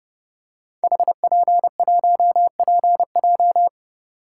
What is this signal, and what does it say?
Morse code, heard as a keyed tone, about 30 wpm, 700 Hz. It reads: HP1PJ
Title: Code Groups - 5 Letter:Numbers - 30wpm